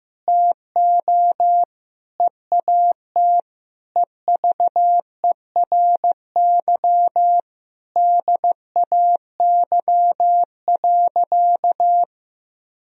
Morse code, heard as a keyed tone, about 15 wpm, 700 Hz. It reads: TO EAT EVERY DAY.
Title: Sentences from Top 1000 Words - Review Part 4 - 15wpm